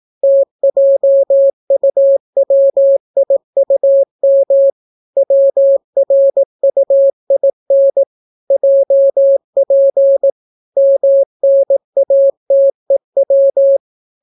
Morse code, heard as a keyed tone, about 18 wpm, 550 Hz. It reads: TJUWIUM WRUIN JP MNATEW